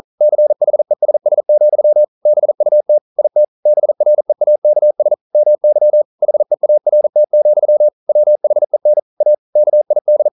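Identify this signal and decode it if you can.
Morse code, heard as a keyed tone, about 42 wpm, 600 Hz. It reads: CHESS, BUT IT BREAKS MY HEART, WHEN A KID
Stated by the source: Morse code practice at 41 wpm, from CW med Gnister.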